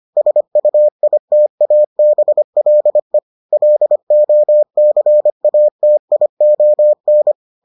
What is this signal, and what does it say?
Morse code, heard as a keyed tone, about 25 wpm, 600 Hz. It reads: SUITABLE LOCATION